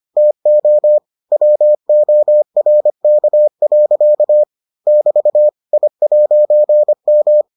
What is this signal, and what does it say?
Morse code, heard as a keyed tone, about 25 wpm, 600 Hz. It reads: TO WORK. <BT> I'M